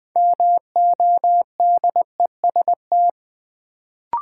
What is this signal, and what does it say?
Morse code, heard as a keyed tone, about 20 wpm, 700 Hz. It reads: MODEST
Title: Build A Word - 2 Letters At A Time - 6-Letter Words - 20wpm